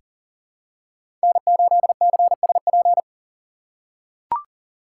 Morse code, heard as a keyed tone, about 40 wpm, 700 Hz. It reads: N8CSP
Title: USA Call Signs - 5 Characters - 4x Character Spacing - 40wpm